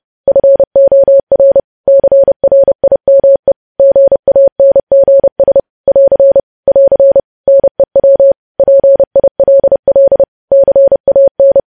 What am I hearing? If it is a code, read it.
Morse code, heard as a keyed tone, about 30 wpm, 550 Hz. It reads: FOR CRIME GANGS <AR> <AR> NEW PILL CAN
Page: news headlines in morse code at 30 wpm